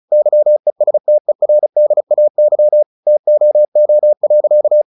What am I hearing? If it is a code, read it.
Morse code, heard as a keyed tone, about 34 wpm, 600 Hz. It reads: YESTERDAY TOO.
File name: the old man and the sea 2 35wpm